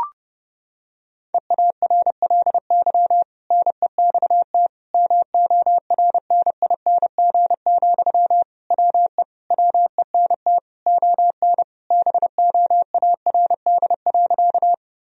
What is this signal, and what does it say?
Morse code, heard as a keyed tone, about 30 wpm, 700 Hz. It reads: EARLY NEXT MORNING, WE WENT ON BOARD.
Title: Sentences from Top 1000 Words - Review Part 4 - 30wpm